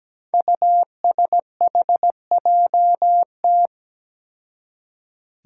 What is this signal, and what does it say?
Morse code, heard as a keyed tone, about 17 wpm, 700 Hz. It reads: USHJT
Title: Five Challenging Letter-Number Combinations 17wpm